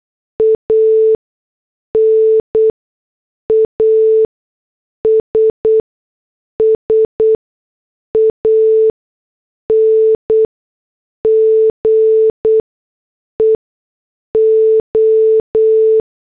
Morse code, heard as a keyed tone, about 8 wpm, spaced as for 6 wpm, 440 Hz. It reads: AN ASSANGE O
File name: morse / headlines